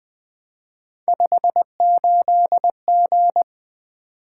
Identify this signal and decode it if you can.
Morse code, heard as a keyed tone, about 20 wpm, 700 Hz. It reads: 58G
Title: Three Letter-Number Combinations - 20wpm